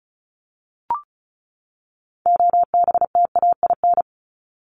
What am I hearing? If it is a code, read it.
Morse code, heard as a keyed tone, about 34 wpm, 700 Hz. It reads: OBTAIN
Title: ED Suffix - Top 100 Words with Base Words 35wpm